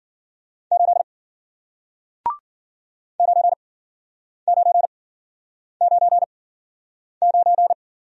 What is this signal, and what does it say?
Morse code, heard as a keyed tone, about 53 wpm, spaced as for 9 wpm, 700 Hz. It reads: 8 9999